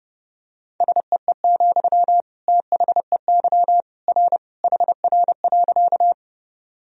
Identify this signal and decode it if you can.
Morse code, heard as a keyed tone, about 32 wpm, 700 Hz. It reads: SEE, THEY R HR.
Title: Sentences from Top 200 Words and Abbreviations 30wpm